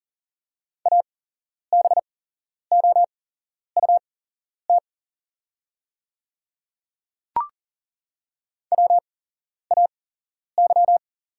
Morse code, heard as a keyed tone, about 41 wpm, spaced as for 11 wpm, 700 Hz. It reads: ABOUT WAY
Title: 8x Character Spacing - Top 100 Words - 40wpm